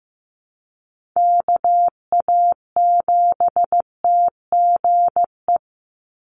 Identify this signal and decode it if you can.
Morse code, heard as a keyed tone, about 15 wpm, 700 Hz. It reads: KA7TGE
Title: USA Call Signs 15wpm